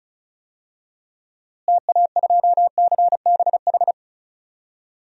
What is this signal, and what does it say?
Morse code, heard as a keyed tone, about 36 wpm, 700 Hz. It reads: TA2CBH
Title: DXCC-100 Call Signs - 6 Characters - Repeated 2x - 35wpm